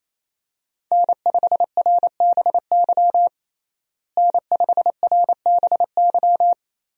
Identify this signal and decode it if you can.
Morse code, heard as a keyed tone, about 28 wpm, 700 Hz. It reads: N5RBY N5RBY
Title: USA Call Signs - 5 Characters - Repeated 3x - 28wpm